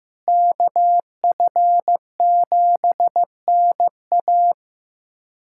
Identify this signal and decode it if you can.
Morse code, heard as a keyed tone, about 16 wpm, 700 Hz. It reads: KF7NA
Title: USA Callsign DE Callsign - 5 Characters - 15wpm